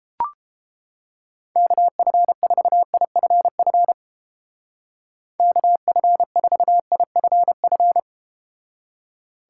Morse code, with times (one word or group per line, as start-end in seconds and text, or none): 1.6-3.9 KF4IFF
5.4-8.0 KF4IFF